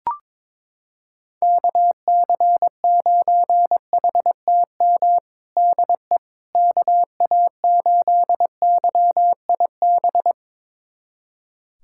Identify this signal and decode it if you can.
Morse code, heard as a keyed tone, about 22 wpm, 700 Hz. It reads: KC9HTM DE KA8YIB
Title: USA Callsign DE Callsign 22wpm